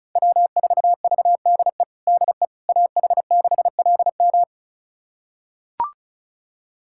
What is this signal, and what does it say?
Morse code, heard as a keyed tone, about 36 wpm, 700 Hz. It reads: W4VDE DE AH6LM
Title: USA Callsign DE Callsign - 5 Characters - 35wpm